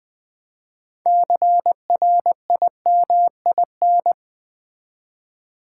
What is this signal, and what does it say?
Morse code, heard as a keyed tone, about 20 wpm, 700 Hz. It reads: CRIMIN
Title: Build A Word - 3 Letters At A Time - 8-Letter Words - 20wpm